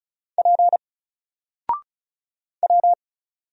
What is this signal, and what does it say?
Morse code, heard as a keyed tone, about 34 wpm, 700 Hz. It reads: P W
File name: Single-Letter-Number-35wpm